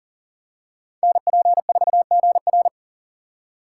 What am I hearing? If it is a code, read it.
Morse code, heard as a keyed tone, about 40 wpm, 700 Hz. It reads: NP4GR
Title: USA Call Signs - 5 Characters - 4x Character Spacing - 40wpm